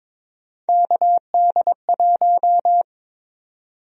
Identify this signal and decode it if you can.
Morse code, heard as a keyed tone, about 22 wpm, 700 Hz. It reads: KD1
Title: USA Call Sign Prefixes - 22wpm